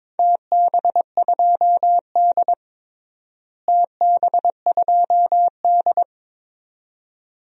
Morse code, heard as a keyed tone, about 22 wpm, 700 Hz. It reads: TB2D TB2D
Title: DXCC-100 Call Signs - 4 Characters - Repeated 2x - 22wpm